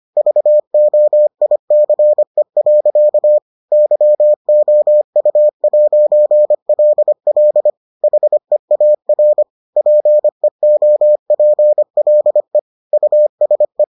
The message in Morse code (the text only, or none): VOICE. YOU'LL HEAR PEOPLE USE